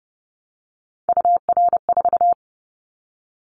Morse code, heard as a keyed tone, about 30 wpm, 690 Hz. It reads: UR4
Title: Callsign Prefixes for Top 25 DXCC - Repeated 3x - 30wpm